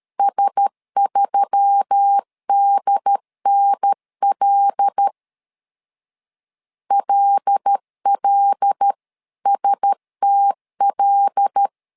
Morse code, heard as a keyed tone, about 13 wpm, 800 Hz. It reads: S3DNL LLSTL
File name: LEZIONI / 5 LBVC / 5 L 12wpm 1913